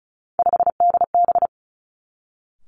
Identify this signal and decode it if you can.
Morse code, heard as a keyed tone, about 36 wpm, 700 Hz. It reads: 5DB
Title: Three Challenging Letter-Number Combinations 35wpm